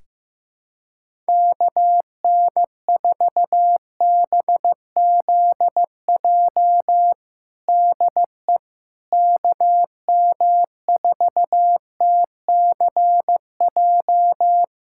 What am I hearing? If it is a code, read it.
Morse code, heard as a keyed tone, about 15 wpm, 700 Hz. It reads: KN4BZJ DE KM4TCJ
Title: USA Callsign DE Callsign - 6 Characters - 15wpm